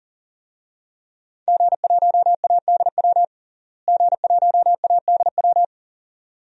Morse code, heard as a keyed tone, about 40 wpm, 700 Hz. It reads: G1ADW G1ADW
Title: DXCC-100 Call Signs - 5 Characters - Repeated 2x - 40wpm